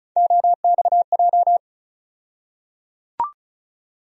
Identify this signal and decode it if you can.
Morse code, heard as a keyed tone, about 36 wpm, 700 Hz. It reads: OXJ
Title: USA Call Sign Postfixes - 3 Letters 35wpm